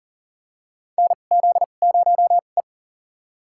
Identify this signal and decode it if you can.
Morse code, heard as a keyed tone, about 40 wpm, spaced as for 29 wpm, 700 Hz. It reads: NZ0E
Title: USA Call Signs - 4 Characters - 2x Character Spacing - 40wpm